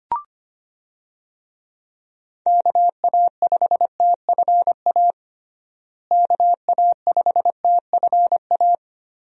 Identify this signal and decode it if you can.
Morse code, heard as a keyed tone, about 25 wpm, 700 Hz. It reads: KA5TFA KA5TFA
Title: USA Call Signs - 6 Characters - Repeated 2x - 25wpm